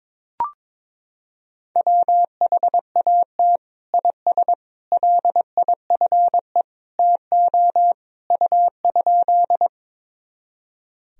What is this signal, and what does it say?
Morse code, heard as a keyed tone, about 22 wpm, 700 Hz. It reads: WHAT IS LIFE TO U?
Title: Sentences from Top 300 Words and Abbreviations 22wpm